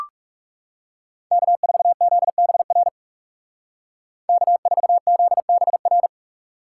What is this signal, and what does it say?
Morse code, heard as a keyed tone, about 39 wpm, 700 Hz. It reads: K4ZBR K4ZBR